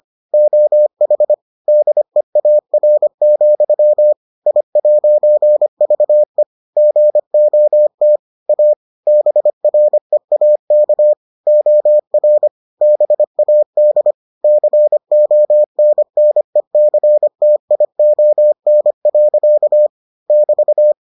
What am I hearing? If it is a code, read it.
Morse code, heard as a keyed tone, about 25 wpm, 600 Hz. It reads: OH DEAR, I'VE GOT A BREAK OR BAD CONNECTION. <BT>